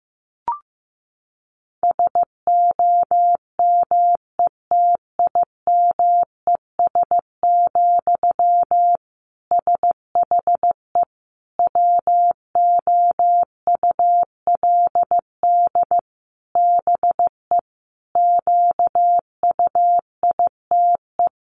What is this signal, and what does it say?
Morse code, heard as a keyed tone, about 14 wpm, 700 Hz. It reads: SOMETIMES, SHE WOULD BE QUITE
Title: Sentences from Top 1000 Words 15wpm